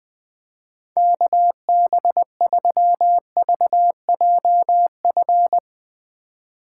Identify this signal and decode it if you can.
Morse code, heard as a keyed tone, about 20 wpm, 700 Hz. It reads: KB3VJF
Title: Speed Racing - USA 2x3 Call Signs 17wpm